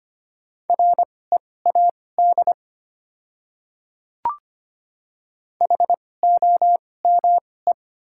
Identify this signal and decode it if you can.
Morse code, heard as a keyed tone, about 25 wpm, spaced as for 18 wpm, 690 Hz. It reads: READ HOME
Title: 2x Character Spacing - Top 200 Words - 25wpm